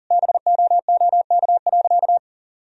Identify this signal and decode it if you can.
Morse code, heard as a keyed tone, about 40 wpm, 700 Hz. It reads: BOOK.
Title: Sentences from Top 1000 Words - Review Part 3 - 40wpm